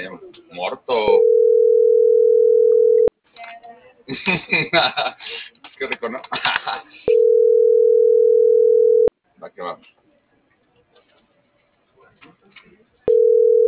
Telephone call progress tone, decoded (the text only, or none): ringback tone